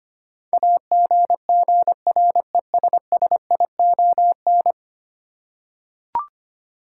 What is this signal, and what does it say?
Morse code, heard as a keyed tone, about 25 wpm, 700 Hz. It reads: AGGRESSION
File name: 10 Letter Words - First 3 Letters Given - 25wpm